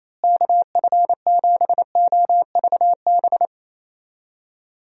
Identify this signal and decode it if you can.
Morse code, heard as a keyed tone, about 28 wpm, 700 Hz. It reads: KF7OVB
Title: USA Call Signs - 6 Characters - Repeated 2x - 28wpm